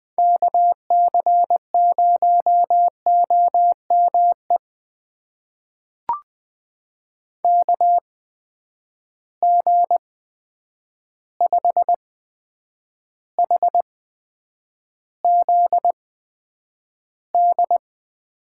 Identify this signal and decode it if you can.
Morse code, heard as a keyed tone, about 20 wpm, spaced as for 9 wpm, 700 Hz. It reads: KC0OME KG5HZD
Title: USA Call Signs - 6 Characters - 8x Character Spacing - 20wpm